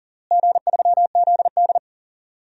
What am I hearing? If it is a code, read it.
Morse code, heard as a keyed tone, about 40 wpm, 700 Hz. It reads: G3ZD
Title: DXCC-100 Call Signs - 4 Characters - Repeated 3x - 40wpm